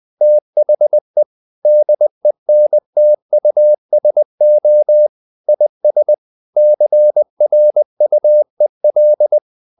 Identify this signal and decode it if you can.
Morse code, heard as a keyed tone, about 20 wpm, 600 Hz. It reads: THE DENTUSO IS CRUEL
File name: the old man and the sea part 2 20wpm